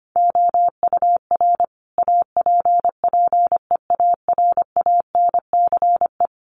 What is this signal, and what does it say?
Morse code, heard as a keyed tone, about 25 wpm, 700 Hz. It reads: OUR APPEARANCE
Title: Sentences from Top 1100 Words 25wpm